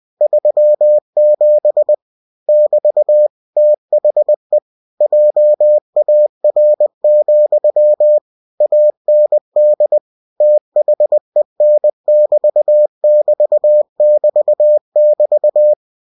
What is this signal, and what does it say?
Morse code, heard as a keyed tone, about 20 wpm, 600 Hz. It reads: 37 <BT> THE JAR, AND THEN<BT><BT><BT><BT>
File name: cw-books / winnie-the-pooh / winnie-20wpm-page-0037